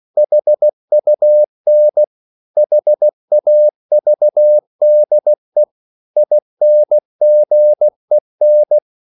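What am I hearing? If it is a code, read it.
Morse code, heard as a keyed tone, about 16 wpm, 600 Hz. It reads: HUN HAVDE INGEN
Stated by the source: Slow (QRS) Morse code at 16 wpm, from CW med Gnister.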